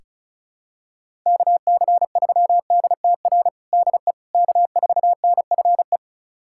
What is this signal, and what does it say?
Morse code, heard as a keyed tone, about 36 wpm, 700 Hz. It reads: KC3DTR DE K4NFE
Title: USA Callsign DE Callsign 35wpm